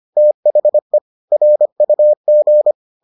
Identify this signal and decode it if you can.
Morse code, heard as a keyed tone, about 25 wpm, 600 Hz. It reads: THE RUG